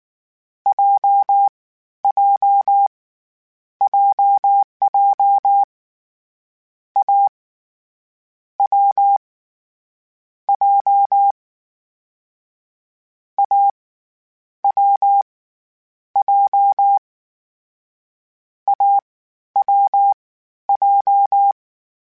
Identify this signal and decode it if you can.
Morse code, heard as a keyed tone, about 19 wpm, spaced as for 12 wpm, 800 Hz. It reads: JJ JJ A W J A W J AWJ
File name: Difficult Combos / Difficult Combos-mp3s / AWJ ShortRun@18